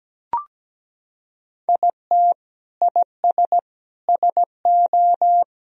Morse code, heard as a keyed tone, about 17 wpm, 700 Hz. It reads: IT IS SO
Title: Sentences from Top 300 Words and Abbreviations 17wpm